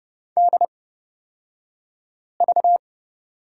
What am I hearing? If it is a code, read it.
Morse code, heard as a keyed tone, about 30 wpm, spaced as for 5 wpm, 700 Hz. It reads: DV
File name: Lesson 38d - Letter q - Two Characters 30wpm